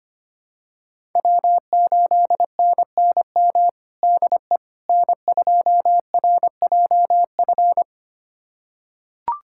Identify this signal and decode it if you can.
Morse code, heard as a keyed tone, about 25 wpm, 700 Hz. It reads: W8NNM DE N2RJF
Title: USA Callsign DE Callsign - 5 Characters - 25wpm